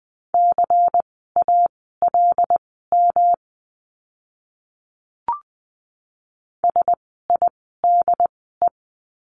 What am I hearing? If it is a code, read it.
Morse code, heard as a keyed tone, about 20 wpm, spaced as for 15 wpm, 700 Hz. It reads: CALM SIDE
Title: 108 Lesson 18f - Letter m - Words 20wpm